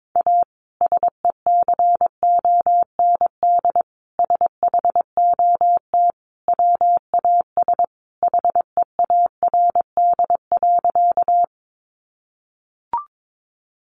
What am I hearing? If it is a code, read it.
Morse code, heard as a keyed tone, about 22 wpm, 700 Hz. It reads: A SECOND SHOT WAS HEARD.
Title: Sentences from Top 1100 Words 22wpm